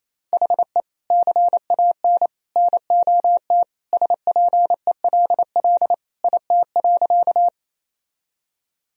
Morse code, heard as a keyed tone, about 28 wpm, 700 Hz. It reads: HE CAN NOT SPELL IT.
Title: Sentences from Top 1000 Words - Review Part 1 - 28wpm